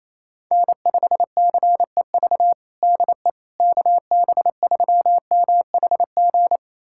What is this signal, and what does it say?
Morse code, heard as a keyed tone, about 28 wpm, 700 Hz. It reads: N5CEV DE KB3MHG
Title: USA Callsign DE Callsign 28wpm